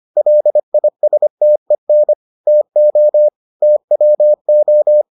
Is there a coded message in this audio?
Yes: Morse code from 0.2 to 5.0 s, heard as a keyed tone, about 25 wpm, 600 Hz. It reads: LISTEN TO TWO